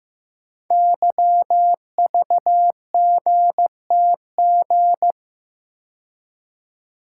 Morse code, heard as a keyed tone, about 15 wpm, 700 Hz. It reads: YVGTG
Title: Code Groups - 5 Letters - 15wpm